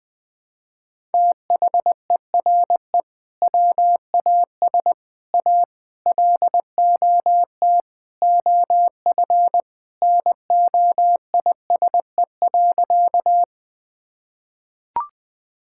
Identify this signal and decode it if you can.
Morse code, heard as a keyed tone, about 21 wpm, 700 Hz. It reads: THERE WAS A LOT OF NOISE.
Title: Sentences with 4-Word Phrases - 20wpm